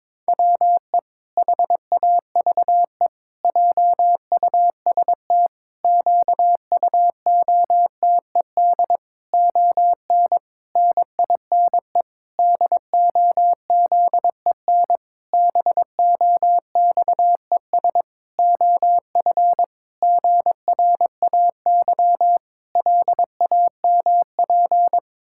Morse code, heard as a keyed tone, about 22 wpm, 700 Hz. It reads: WE HAVE JUST QUOTED ON NINE DOZEN BOXES OF GRAY LAMP
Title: Pangram Challenge v2 - 22wpm